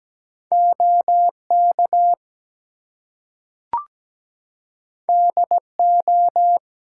Morse code, heard as a keyed tone, about 17 wpm, 700 Hz. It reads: OK DO
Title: Two Letter Words - 17wpm